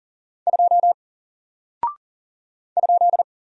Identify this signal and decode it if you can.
Morse code, heard as a keyed tone, about 40 wpm, 700 Hz. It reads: 2 ?
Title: Single Character (Letter:Number:Punctuation:Prosign) Alt 40wpm